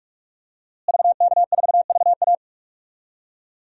Morse code, heard as a keyed tone, about 44 wpm, 700 Hz. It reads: VK4VA
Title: DXCC-100 Call Signs - 5 Characters - Repeated 3x - 45wpm